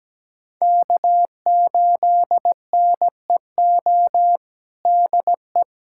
Morse code, heard as a keyed tone, about 17 wpm, 700 Hz. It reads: K8NEO DE
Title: USA Callsign DE Callsign 17wpm